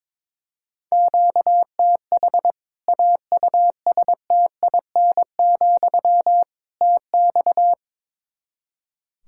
Morse code, heard as a keyed tone, about 22 wpm, 700 Hz. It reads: QTH AUSTIN, TX
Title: Practice QSO - Exchange 1 - Beginning - 22wpm